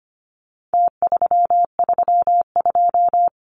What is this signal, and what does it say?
Morse code, heard as a keyed tone, about 25 wpm, 700 Hz. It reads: T332
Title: Numbers - Contest Style - 4-Digit 25wpm